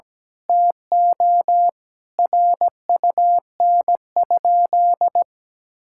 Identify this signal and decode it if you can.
Morse code, heard as a keyed tone, about 17 wpm, 700 Hz. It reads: TO RUN?
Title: Sentences from Top 1000 Words - Review Part 6 - 17wpm